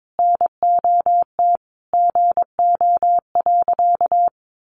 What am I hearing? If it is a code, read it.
Morse code, heard as a keyed tone, about 22 wpm, 700 Hz. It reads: NOT GO.
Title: Sentences from Top 800 Words 22wpm